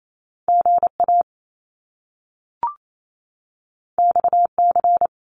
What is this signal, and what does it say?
Morse code, heard as a keyed tone, about 28 wpm, 700 Hz. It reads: GA XC
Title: Top Two Letter Combinations in Words 28wpm